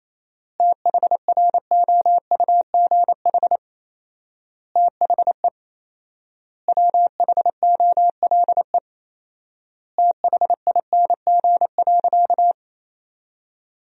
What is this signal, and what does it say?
Morse code, heard as a keyed tone, about 27 wpm, 700 Hz. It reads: THROUGH THE WHOLE THING.
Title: 4x Word-Spacing - Sentences from Top 400 Words - 28wpm